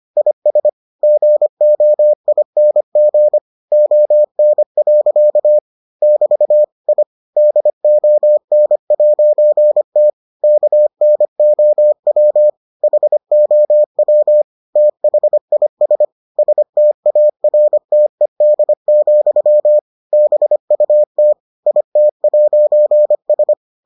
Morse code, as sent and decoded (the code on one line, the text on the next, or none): .. ... / --. --- .. -. --. / --- -. .-.-.- / -...- / .. / -.. --- -. .----. - / -.- -. --- .-- / .... --- .-- / - .... .. ... / ... - .- .-. - . -.. --..-- / -... ..- - / .. - .----. ...
IS GOING ON. <BT> I DON'T KNOW HOW THIS STARTED, BUT IT'S